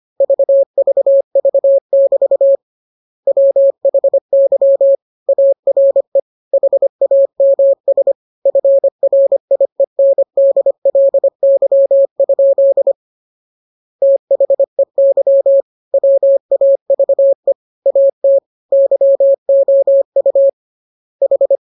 Morse code, heard as a keyed tone, about 25 wpm, 550 Hz. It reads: VVV<BT> WHY ARE HAMS FRIENDLY? THEY WAVE AT YOU H